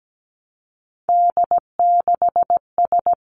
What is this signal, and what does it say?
Morse code, heard as a keyed tone, about 17 wpm, 700 Hz. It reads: D6S
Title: Three Challenging Letter-Number Combinations 17wpm